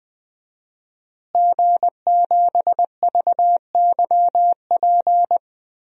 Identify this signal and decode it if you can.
Morse code, heard as a keyed tone, about 20 wpm, 700 Hz. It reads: G7VYP
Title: DXCC-100 Call Signs - 5 Characters - Repeated 2x - 20wpm